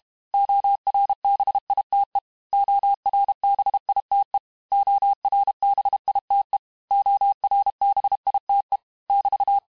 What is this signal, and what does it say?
Morse code, heard as a keyed tone, about 32 wpm, 800 Hz. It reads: ORBITE ORBITE ORBITE ORBITE <BT>
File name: cours-du-mercredi / audios / cours-868-teste-MOTS-vitesse-30mpm